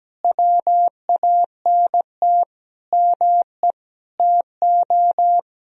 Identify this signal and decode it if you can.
Morse code, heard as a keyed tone, about 17 wpm, 700 Hz. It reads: WANT ME TO
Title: Sentences from Top 200 Words and Abbreviations 17wpm